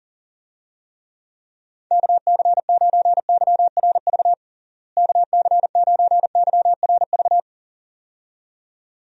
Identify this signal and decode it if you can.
Morse code, heard as a keyed tone, about 37 wpm, 700 Hz. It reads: KC9YRV KC9YRV
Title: USA Call Signs - 6 Characters - Repeated 2x - 40wpm